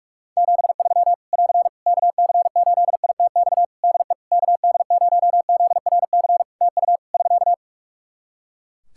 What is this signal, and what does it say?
Morse code, heard as a keyed tone, about 44 wpm, 700 Hz. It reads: 73 <AR> KC8ITX DE KD0ZRC TU <SK>
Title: Practice QSO - Exchange 4 - Ending - 45wpm